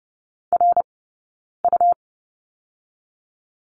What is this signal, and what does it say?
Morse code, heard as a keyed tone, about 30 wpm, 700 Hz. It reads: R U
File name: Lesson 19d - Letter w - Two Characters 30wpm